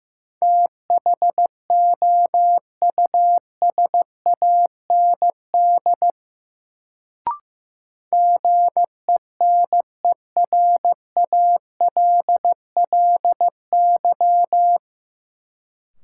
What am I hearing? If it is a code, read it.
Morse code, heard as a keyed tone, about 15 wpm, 700 Hz. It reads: THOUSAND GENERALLY
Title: Sentences from Top 600 Words - New Word Drill - 15wpm